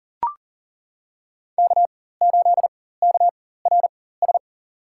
Morse code, heard as a keyed tone, about 38 wpm, spaced as for 18 wpm, 700 Hz. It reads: K8KRS
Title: USA Call Signs - 5 Characters - 4x Character Spacing - 40wpm